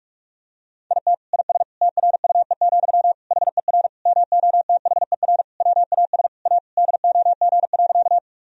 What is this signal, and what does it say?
Morse code, heard as a keyed tone, about 46 wpm, 700 Hz. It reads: IT IS TRUE, HER MOTHER WAS A DOG.
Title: Sentences from Top 1000 Words - Review Part 2 - 45wpm